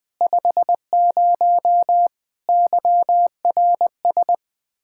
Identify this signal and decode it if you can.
Morse code, heard as a keyed tone, about 20 wpm, 700 Hz. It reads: 50 YRS
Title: Practice QSO - Exchange 3 - More Info - 20wpm